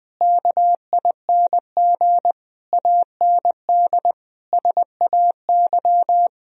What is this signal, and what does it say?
Morse code, heard as a keyed tone, about 21 wpm, 700 Hz. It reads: KING AND SAY
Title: Sentences from Top 1000 Words - Review Part 4 - 20wpm